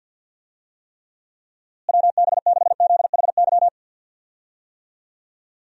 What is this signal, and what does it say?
Morse code, heard as a keyed tone, about 52 wpm, 700 Hz. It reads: WB6ZHY
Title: USA Call Signs - 6 Characters - Repeated 3x - 50wpm